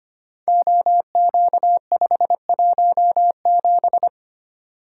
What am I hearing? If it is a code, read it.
Morse code, heard as a keyed tone, about 25 wpm, 700 Hz. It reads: OQ517
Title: Code Groups - 5 Letter:Numbers - 25wpm